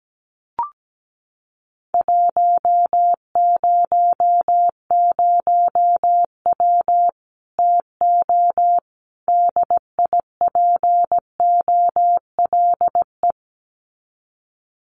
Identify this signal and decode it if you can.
Morse code, heard as a keyed tone, about 17 wpm, 700 Hz. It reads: 100W TO DIPOLE
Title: Common QSO Phrases 17wpm